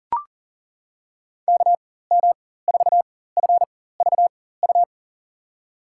Morse code, heard as a keyed tone, about 38 wpm, spaced as for 18 wpm, 700 Hz. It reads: KM4FVU
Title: USA Call Signs - 6 Characters - 4x Character Spacing - 40wpm